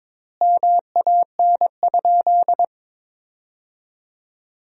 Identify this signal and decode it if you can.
Morse code, heard as a keyed tone, about 22 wpm, 700 Hz. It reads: MAN?
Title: 3x Word-Spacing - Sentences from Top 200 Words - 22wpm